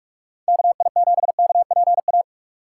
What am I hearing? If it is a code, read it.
Morse code, heard as a keyed tone, about 43 wpm, 700 Hz. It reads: KI7KPA
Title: USA Call Signs - 6 Characters - 6x Character Spacing - 45wpm